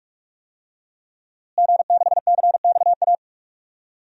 Morse code, heard as a keyed tone, about 46 wpm, 700 Hz. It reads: G6CXA
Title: DXCC-100 Call Signs - 5 Characters - Repeated 2x - 45wpm